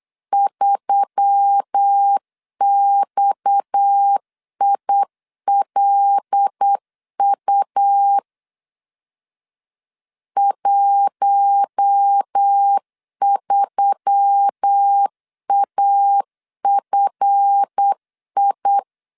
Morse code, heard as a keyed tone, about 8 wpm, 800 Hz. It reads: 3XILU 13AFI
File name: LEZIONI / 7 GF9Z / 7 GF 8wpm 3108